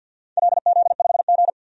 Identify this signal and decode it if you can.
Morse code, heard as a keyed tone, about 51 wpm, 700 Hz. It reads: LZ5G
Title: DXCC-100 Call Signs - 4 Characters - 50wpm